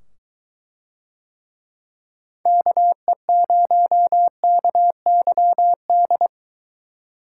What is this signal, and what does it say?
Morse code, heard as a keyed tone, about 23 wpm, 700 Hz. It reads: KE0KYD